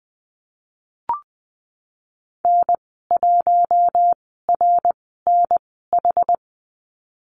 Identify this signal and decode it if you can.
Morse code, heard as a keyed tone, about 20 wpm, spaced as for 14 wpm, 700 Hz. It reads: N1RNH
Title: 069 Lesson 12g - Letter d - Call Signs 20wpm